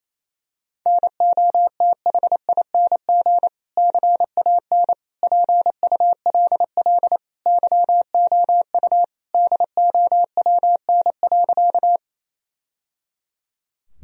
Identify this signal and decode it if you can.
Morse code, heard as a keyed tone, about 27 wpm, 700 Hz. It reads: NOTHING CAN PULL YOU DOWN.
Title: Sentences from Top 1000 Words - Review Part 2 - 28wpm